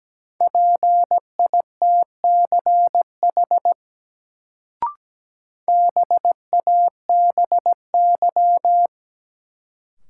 Morse code, heard as a keyed tone, about 17 wpm, 700 Hz. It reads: PITCH BABY
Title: Sentences from Top 1300 Words - New Word Drill Alt - 17wpm